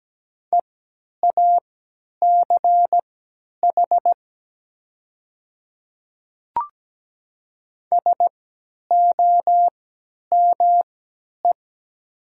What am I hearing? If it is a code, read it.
Morse code, heard as a keyed tone, about 17 wpm, spaced as for 10 wpm, 700 Hz. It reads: EACH SOME
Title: 3x Character Spacing - Top 100 Words - 17wpm